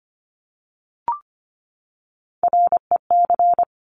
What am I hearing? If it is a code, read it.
Morse code, heard as a keyed tone, about 25 wpm, 700 Hz. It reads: REC
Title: Top Three Letter Combinations in Words 25wpm